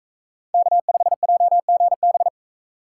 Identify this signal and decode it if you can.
Morse code, heard as a keyed tone, about 42 wpm, 700 Hz. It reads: K5JGB